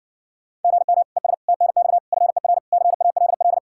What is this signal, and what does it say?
Morse code, heard as a keyed tone, about 60 wpm, 700 Hz. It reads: GM ES TNX FER CALL